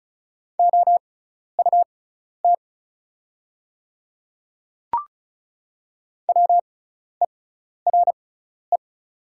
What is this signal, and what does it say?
Morse code, heard as a keyed tone, about 36 wpm, spaced as for 12 wpm, 700 Hz. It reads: OUT WERE
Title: 6x Character Spacing - Top 100 Words - 35wpm